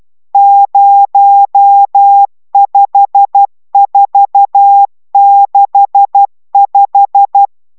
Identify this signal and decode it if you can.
Morse code, heard as a keyed tone, about 12 wpm, 800 Hz. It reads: 05465